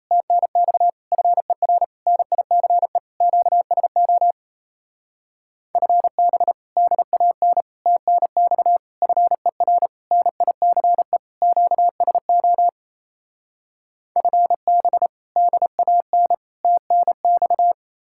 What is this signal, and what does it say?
Morse code, heard as a keyed tone, about 38 wpm, 700 Hz. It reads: TNX FER NICE QSO FB DAN TNX FER NICE QSO FB DAN TNX